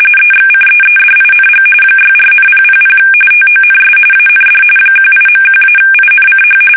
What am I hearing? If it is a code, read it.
RTTY, about 45.45 baud. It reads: EXCEEDING 64,000 TONS AND 18.1-INCH
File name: fsk850 / battleshipnj